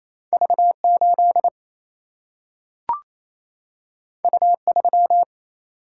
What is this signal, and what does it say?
Morse code, heard as a keyed tone, about 28 wpm, 700 Hz. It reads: V8 U3
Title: Single-Letter Single-Number 28wpm